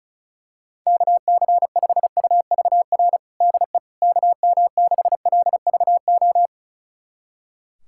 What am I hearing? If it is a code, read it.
Morse code, heard as a keyed tone, about 34 wpm, 700 Hz. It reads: KC5UVR DE KM6LVO
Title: USA Callsign DE Callsign 35wpm